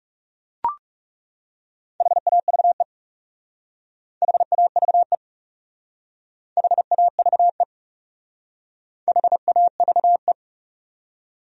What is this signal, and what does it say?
Morse code, heard as a keyed tone, about 39 wpm, 700 Hz. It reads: HAVE HAVE HAVE HAVE